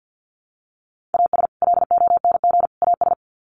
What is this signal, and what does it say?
Morse code, heard as a keyed tone, about 48 wpm, 700 Hz. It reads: AS LONG AS
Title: Sets of 3 Words 50wpm